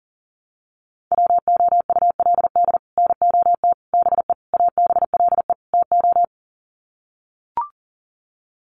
Morse code, heard as a keyed tone, about 40 wpm, 700 Hz. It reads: WOULD NOT BE ABLE TO
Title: Sets of 5 Words 40wpm